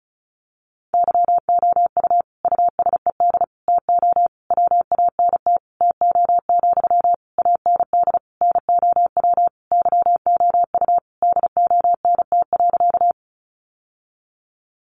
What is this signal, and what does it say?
Morse code, heard as a keyed tone, about 34 wpm, 700 Hz. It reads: YOU USED TO WANT TO, AND NOW YOU DONT.
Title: Sentences from Top 600 Words 35wpm